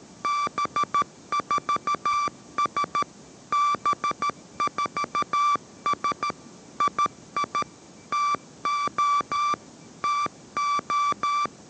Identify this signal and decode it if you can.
Morse code, heard as a keyed tone, about 13 wpm, 1170 Hz. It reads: B4S B4S II TO TO